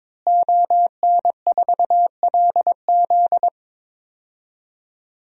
Morse code, heard as a keyed tone, about 22 wpm, 700 Hz. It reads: ON4LZ
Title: DXCC-100 Call Signs - 5 Characters - Repeated 3x - 22wpm